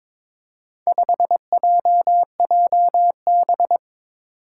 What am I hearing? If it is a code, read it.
Morse code, heard as a keyed tone, about 22 wpm, 700 Hz. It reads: 5JJB
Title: Four Challenging Letter-Number Combinations 22wpm